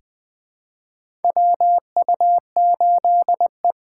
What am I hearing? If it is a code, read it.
Morse code, heard as a keyed tone, about 20 wpm, 700 Hz. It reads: WU8E
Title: 2023 POTA Hunter Call Signs - 20wpm